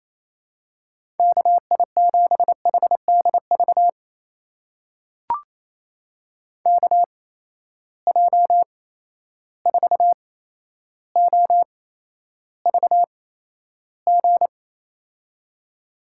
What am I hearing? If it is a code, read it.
Morse code, heard as a keyed tone, about 28 wpm, spaced as for 12 wpm, 700 Hz. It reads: KI7HDV KJ4OVG
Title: USA Call Signs - 6 Characters - 8x Character Spacing - 28wpm